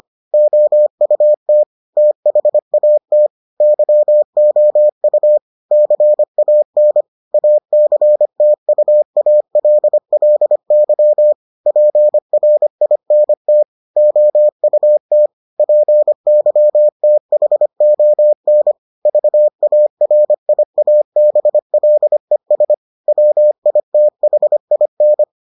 Morse code, heard as a keyed tone, about 25 wpm, 600 Hz. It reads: OUT THAT YOU CAN ACTUALLY PRINT OUT PYTHON VARIABLES WITHIN